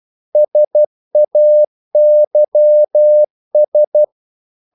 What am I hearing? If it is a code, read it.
Morse code, heard as a keyed tone, about 12 wpm, 600 Hz. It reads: SAYS
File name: CW / Practice / If12